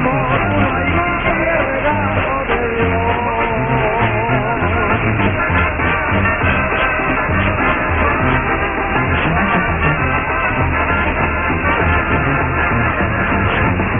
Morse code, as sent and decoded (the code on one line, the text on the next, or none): none